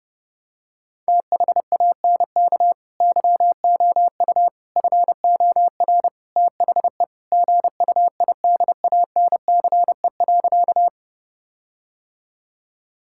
Morse code, heard as a keyed tone, about 30 wpm, 700 Hz. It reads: THANK YOU FOR THE GUIDANCE.
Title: Sentences from Top 1600 Words 30wpm